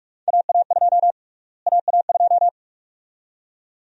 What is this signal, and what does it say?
Morse code, heard as a keyed tone, about 46 wpm, 700 Hz. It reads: AA2 AA2
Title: USA Call Sign Prefixes - Repeated 2x - 45wpm